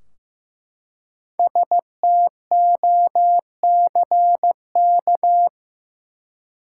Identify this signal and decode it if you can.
Morse code, heard as a keyed tone, about 15 wpm, 700 Hz. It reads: STOCK
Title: Five Letter Words - 15wpm